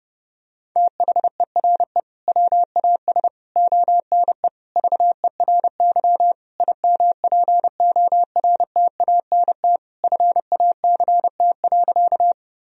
Morse code, heard as a keyed tone, about 30 wpm, 700 Hz. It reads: THERE WAS ONE VERY IMPORTANT FACT.
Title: Sentences from Top 1000 Words - Review Part 3 - 30wpm